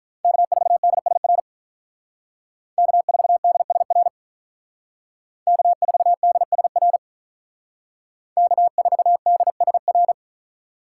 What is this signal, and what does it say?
Morse code, heard as a keyed tone, about 45 wpm, 700 Hz. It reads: K4DSR K4DSR K4DSR K4DSR